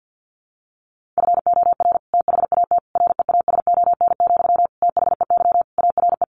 Morse code, heard as a keyed tone, about 50 wpm, 700 Hz. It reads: FOR THAT REASON, THEY ARE
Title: Sentences from Top 1500 Words 50wpm